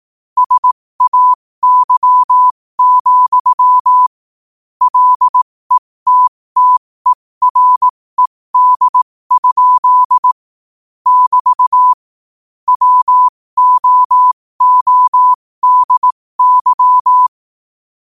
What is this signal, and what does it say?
Morse code, heard as a keyed tone, about 18 wpm, spaced as for 15 wpm, 1000 Hz. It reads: SAY, LETTERED? <BT> WOODY
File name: qotdfiles / 15wpm